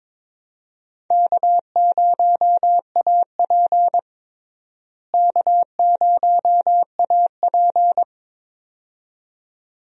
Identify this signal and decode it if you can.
Morse code, heard as a keyed tone, about 22 wpm, 700 Hz. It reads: K0AP K0AP
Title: USA Call Signs - 4 Characters - Repeated 2x - 22wpm